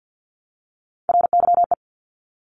Morse code, heard as a keyed tone, about 50 wpm, 690 Hz. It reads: RYE
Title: Multi-letter Phonemes Repeated 3x 50wpm